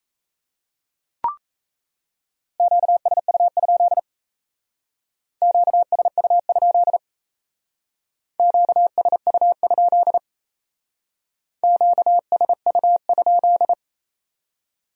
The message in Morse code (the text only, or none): QSU? QSU? QSU? QSU?